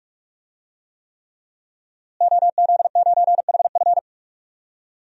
Morse code, heard as a keyed tone, about 44 wpm, 700 Hz. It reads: OZ9HF
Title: DXCC-100 Call Signs - 5 Characters - Repeated 3x - 45wpm